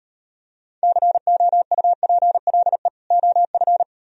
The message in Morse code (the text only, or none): COUPLE OF